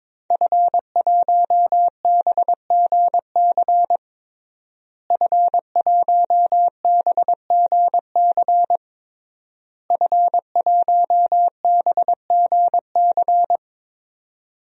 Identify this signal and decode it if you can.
Morse code, heard as a keyed tone, about 21 wpm, 700 Hz. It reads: F1BGC F1BGC F1BGC
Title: DXCC-100 Call Signs - 5 Characters - Repeated 3x - 22wpm